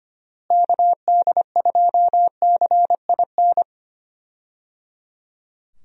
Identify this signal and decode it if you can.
Morse code, heard as a keyed tone, about 25 wpm, 700 Hz. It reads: KD2CIN
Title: USA Call Signs - 6 Characters - Repeated 3x - 25wpm